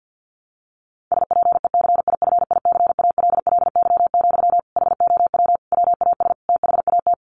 Morse code, heard as a keyed tone, about 50 wpm, 700 Hz. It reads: SPECIFICALLY, HOW WAS THAT
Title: Sentences from Top 1100 Words 50wpm